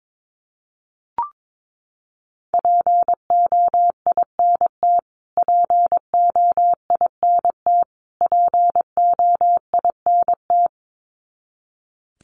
Morse code, heard as a keyed tone, about 22 wpm, 700 Hz. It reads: POINT POINT POINT
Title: Top 500 Words Repeated 22wpm